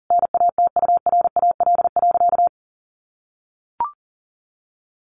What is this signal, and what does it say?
Morse code, heard as a keyed tone, about 40 wpm, 700 Hz. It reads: NATURAL.
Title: Sentences from Top 700 Words 40wpm